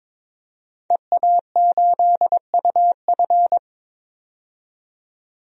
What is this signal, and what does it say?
Morse code, heard as a keyed tone, about 22 wpm, 700 Hz. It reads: EA8UF
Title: DXCC-100 Call Signs - 5 Characters - 22wpm